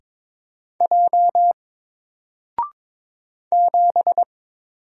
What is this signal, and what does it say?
Morse code, heard as a keyed tone, about 22 wpm, 700 Hz. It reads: J 7
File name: Single-Letter-Number-22wpm